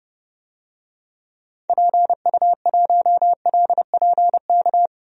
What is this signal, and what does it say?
Morse code, heard as a keyed tone, about 30 wpm, 700 Hz. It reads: PU1LPK
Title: DXCC-100 Call Signs - 6 Characters - 30wpm